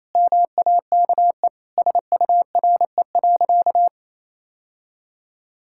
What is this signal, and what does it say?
Morse code, heard as a keyed tone, about 28 wpm, 700 Hz. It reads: MAKE SURE.
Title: Sentences from Top 1000 Words - Review Part 5 - 28wpm